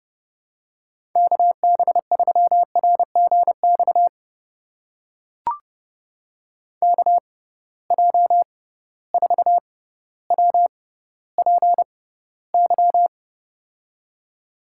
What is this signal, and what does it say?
Morse code, heard as a keyed tone, about 30 wpm, spaced as for 15 wpm, 700 Hz. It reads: KB3RGX KJ4WPY
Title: USA Call Signs - 6 Characters - 6x Character Spacing - 30wpm